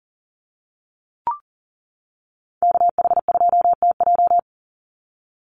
Morse code, heard as a keyed tone, about 38 wpm, 700 Hz. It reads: KH2TJ
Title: NA SOTA Chaser Callsigns 40wpm